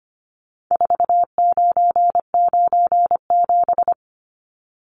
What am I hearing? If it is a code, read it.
Morse code, heard as a keyed tone, about 25 wpm, 700 Hz. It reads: 4997
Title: Numbers - 4-Digit 25wpm